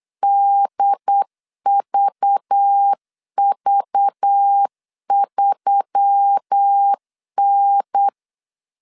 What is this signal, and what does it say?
Morse code, heard as a keyed tone, about 8 wpm, 800 Hz. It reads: DVV3N
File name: CW / DET / 1 9 1500C S A 8wpm 5637